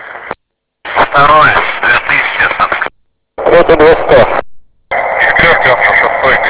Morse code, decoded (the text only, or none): none